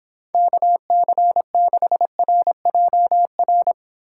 Morse code, heard as a keyed tone, about 25 wpm, 700 Hz. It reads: KC6RJR